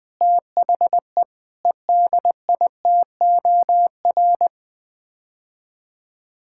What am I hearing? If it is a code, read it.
Morse code, heard as a keyed tone, about 20 wpm, 700 Hz. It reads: THE EDITOR
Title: Sets of 6 Words - 20wpm